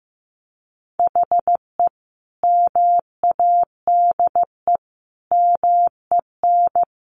Morse code, heard as a keyed tone, about 15 wpm, 700 Hz. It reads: HE MADE MEN
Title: Sentences from Top 1500 Words 15wpm